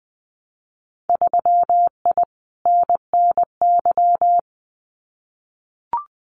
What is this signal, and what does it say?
Morse code, heard as a keyed tone, about 20 wpm, 700 Hz. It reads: 3I NNY
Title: Winter Field Day Exchanges 20wpm